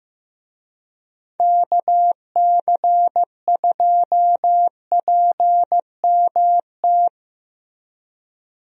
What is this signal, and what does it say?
Morse code, heard as a keyed tone, about 15 wpm, 700 Hz. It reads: KC2PMT
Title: USA Call Signs - 6 Characters - Repeated 2x - 15wpm